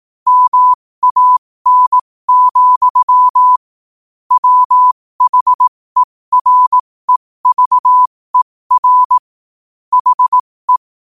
Morse code, heard as a keyed tone, about 18 wpm, spaced as for 15 wpm, 1000 Hz. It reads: MAN, WHEREVER HE
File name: qotdfiles / 15wpm